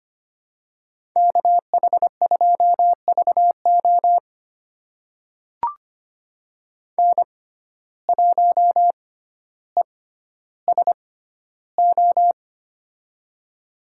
Morse code, heard as a keyed tone, about 25 wpm, spaced as for 13 wpm, 700 Hz. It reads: KH2VO N1ESO
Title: USA Call Signs - 5 Characters - 6x Character Spacing - 25wpm